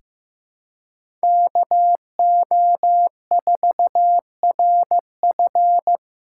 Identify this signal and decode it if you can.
Morse code, heard as a keyed tone, about 16 wpm, 700 Hz. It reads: KO4RF
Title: USA Callsign DE Callsign - 5 Characters - 15wpm